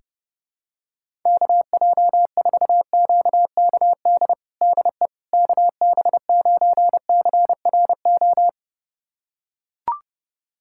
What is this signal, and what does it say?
Morse code, heard as a keyed tone, about 30 wpm, 700 Hz. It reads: KJ4QKD DE KB9CRO
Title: USA Callsign DE Callsign - 6 Characters - 30wpm